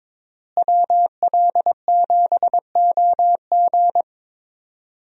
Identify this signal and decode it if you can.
Morse code, heard as a keyed tone, about 22 wpm, 700 Hz. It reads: WL7OG
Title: USA Call Signs - 5 Characters - Repeated 2x - 22wpm